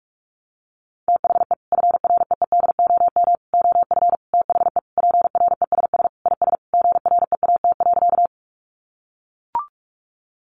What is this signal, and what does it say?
Morse code, heard as a keyed tone, about 44 wpm, 700 Hz. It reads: THE FREEDOM OF THE PRESS IS GREAT.
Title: Sentences from Top 800 Words 45wpm